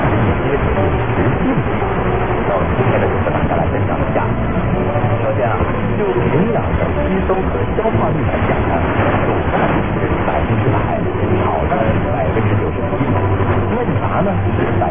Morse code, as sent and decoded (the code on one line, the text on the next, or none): none